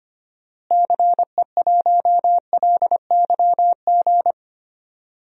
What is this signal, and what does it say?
Morse code, heard as a keyed tone, about 25 wpm, 700 Hz. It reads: CE1LYG
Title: DXCC-100 Call Signs - 6 Characters - Repeated 3x - 25wpm